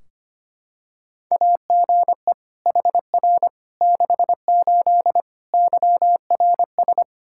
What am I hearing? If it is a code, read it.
Morse code, heard as a keyed tone, about 25 wpm, 700 Hz. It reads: AGE HR 68 YRS
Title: K1USN SST QSO Practice 25wpm